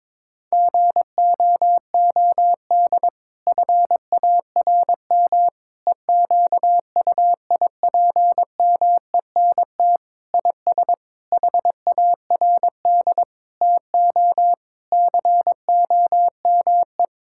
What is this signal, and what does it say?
Morse code, heard as a keyed tone, about 22 wpm, 700 Hz. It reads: GOOD FARM EQUIPMENT IS HARD TO COME
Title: Sentences from Top 1000 Words - Review Part 5 - 22wpm